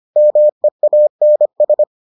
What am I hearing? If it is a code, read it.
Morse code, heard as a keyed tone, about 25 wpm, 600 Hz. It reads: MEANS